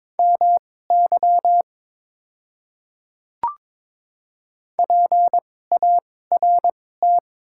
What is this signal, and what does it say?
Morse code, heard as a keyed tone, about 22 wpm, spaced as for 16 wpm, 700 Hz. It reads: MY PART
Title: 2x Character Spacing - Top 300 Words - 22wpm